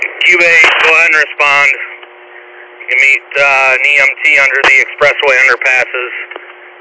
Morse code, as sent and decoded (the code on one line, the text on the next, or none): none